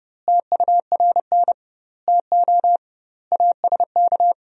Morse code, heard as a keyed tone, about 32 wpm, 700 Hz. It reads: TURN TO ASK
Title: 2x Word-Spacing - Sentences from Top 200 Words - 30wpm